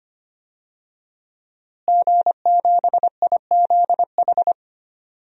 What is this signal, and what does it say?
Morse code, heard as a keyed tone, about 25 wpm, 700 Hz. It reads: G7IZH
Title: DXCC-100 Call Signs - 5 Characters - Repeated 2x - 25wpm